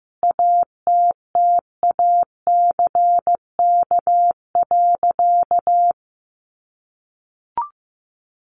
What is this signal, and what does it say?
Morse code, heard as a keyed tone, about 15 wpm, 700 Hz. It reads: ATTACK.
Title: Sentences from Top 1500 Words 15wpm